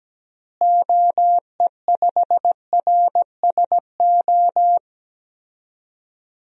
Morse code, heard as a keyed tone, about 17 wpm, 700 Hz. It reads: OE5RSO
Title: DXCC-100 Call Signs - 6 Characters - Repeated 2x - 17wpm